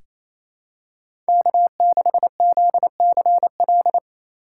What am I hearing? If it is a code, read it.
Morse code, heard as a keyed tone, about 28 wpm, 700 Hz. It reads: K6ZCL
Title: K1USN SST QSO Practice - 28wpm